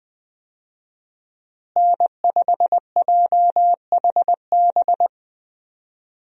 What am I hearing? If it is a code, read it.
Morse code, heard as a keyed tone, about 20 wpm, 700 Hz. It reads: N5JHB
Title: USA Call Signs - 5 Characters - Repeated 3x - 20wpm